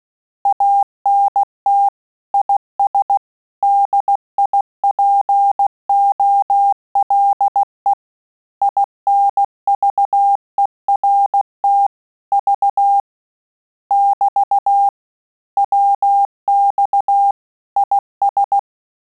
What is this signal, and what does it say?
Morse code, heard as a keyed tone, about 16 wpm, 800 Hz. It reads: ANT IS DIPOLE INVERT V <BT> WX IS
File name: Telegraphy-group / QSO CW 15 PPM